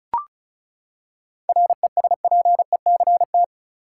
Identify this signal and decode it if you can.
Morse code, heard as a keyed tone, about 36 wpm, 700 Hz. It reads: RESPECT
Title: Seven Letter Words - 35wpm